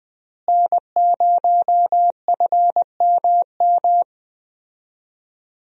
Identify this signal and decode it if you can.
Morse code, heard as a keyed tone, about 21 wpm, 700 Hz. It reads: N0FMM
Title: USA Call Signs - 5 Characters - Repeated 3x - 20wpm